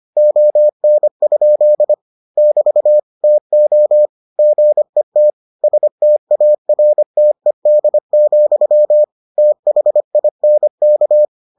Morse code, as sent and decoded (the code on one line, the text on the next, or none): --- -. ..--.. / -...- / - --- / --. - / ... - .- .-. - . -.. --..-- / - .... .. -. -.-
ON? <BT> TO GET STARTED, THINK